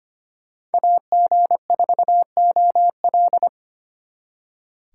Morse code, heard as a keyed tone, about 30 wpm, 700 Hz. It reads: AG4OL
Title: Speed Racing - USA 2x2 Call Signs 25wpm